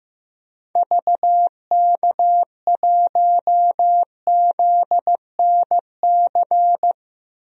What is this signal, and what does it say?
Morse code, heard as a keyed tone, about 15 wpm, 700 Hz. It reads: VK1ZNC
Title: DXCC-100 Call Signs - 6 Characters - 15wpm